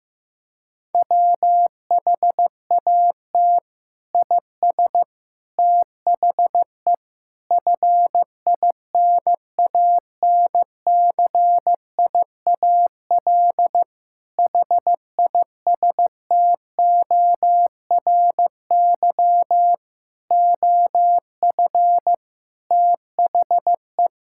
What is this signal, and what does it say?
Morse code, heard as a keyed tone, about 16 wpm, 700 Hz. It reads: WHAT IS THE FINANCIAL HISTORY OF THE
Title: Sentences from Top 1000 Words - Review Part 5 - 15wpm